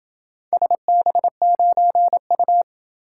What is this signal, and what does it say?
Morse code, heard as a keyed tone, about 28 wpm, 700 Hz. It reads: SB9U